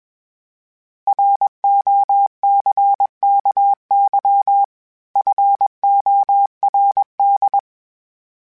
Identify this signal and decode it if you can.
Morse code, heard as a keyed tone, about 21 wpm, 800 Hz. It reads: ROCKY FORD
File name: fromSMs / FromSMs mp3s / Cities@20